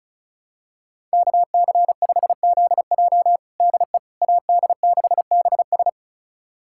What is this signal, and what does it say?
Morse code, heard as a keyed tone, about 34 wpm, 700 Hz. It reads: KC5ZJ DE AD6BS
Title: USA Callsign DE Callsign - 5 Characters - 35wpm